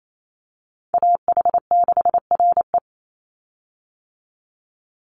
Morse code, heard as a keyed tone, about 28 wpm, 700 Hz. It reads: AH6RE
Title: Call Signs - 28wpm